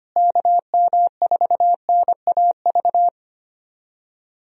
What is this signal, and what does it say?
Morse code, heard as a keyed tone, about 25 wpm, 700 Hz. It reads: KM4NAV
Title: USA Call Signs - 6 Characters - Repeated 3x - 25wpm